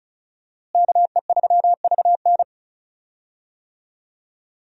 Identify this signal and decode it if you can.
Morse code, heard as a keyed tone, about 36 wpm, 700 Hz. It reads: KE3VN
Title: USA Call Signs - 5 Characters - Repeated 3x - 35wpm